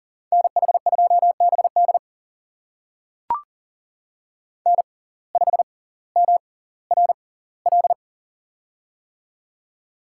Morse code, heard as a keyed tone, about 38 wpm, spaced as for 20 wpm, 700 Hz. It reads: NH2BD N5MRL
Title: USA Call Signs - 5 Characters - 6x Character Spacing - 40wpm